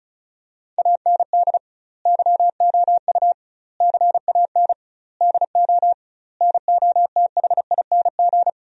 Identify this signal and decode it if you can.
Morse code, heard as a keyed tone, about 34 wpm, 700 Hz. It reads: AND YOU CAN DO NOTHING
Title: 2x Word-Spacing - Sentences from Top 500 Words - 35wpm